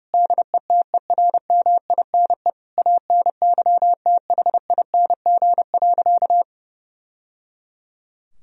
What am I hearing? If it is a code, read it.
Morse code, heard as a keyed tone, about 30 wpm, 700 Hz. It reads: DETERMINE ANYTHING.
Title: Sentences from Top 1000 Words - Review Part 4 - 30wpm